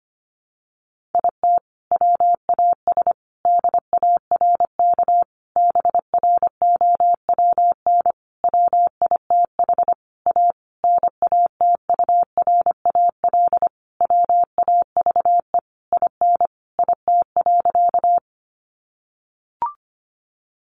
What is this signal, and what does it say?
Morse code, heard as a keyed tone, about 25 wpm, 700 Hz. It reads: IT WAS DARK BROWN WITH A NATURAL WAVE IN IT.
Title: Sentences from Top 1400 Words 25wpm